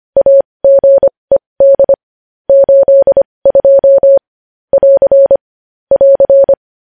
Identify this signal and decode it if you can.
Morse code, heard as a keyed tone, about 25 wpm, spaced as for 20 wpm, 550 Hz. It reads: AGED 82 <AR> <AR>
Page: news-headlines-in-morse-code-at-20-wpm